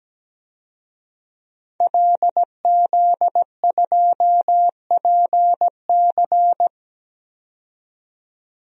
Morse code, heard as a keyed tone, about 17 wpm, 700 Hz. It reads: LZ2PC
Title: DXCC-100 Call Signs - 5 Characters - 17wpm